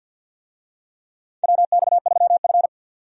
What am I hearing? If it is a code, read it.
Morse code, heard as a keyed tone, about 52 wpm, 700 Hz. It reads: WX3F
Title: USA Call Signs - 4 Characters - 50wpm